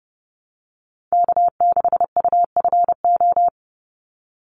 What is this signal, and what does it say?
Morse code, heard as a keyed tone, about 30 wpm, 700 Hz. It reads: K6UFO
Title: Call Signs - 30wpm